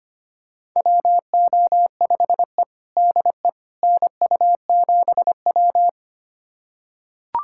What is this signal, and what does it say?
Morse code, heard as a keyed tone, about 25 wpm, 700 Hz. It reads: WO5E DE NU7W
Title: USA Callsign DE Callsign - 4 Characters - 25wpm